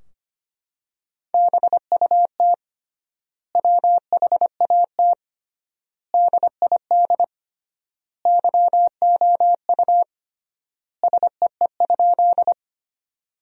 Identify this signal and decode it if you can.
Morse code, heard as a keyed tone, about 25 wpm, 700 Hz. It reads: BUT WHAT DID YOU SEE?
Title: 3x Word-Spacing - Sentences from Top 100 Words - 25wpm